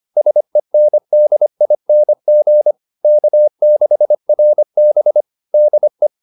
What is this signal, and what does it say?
Morse code, heard as a keyed tone, about 25 wpm, 600 Hz. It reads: SENDING K6RB DE
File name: SS220 25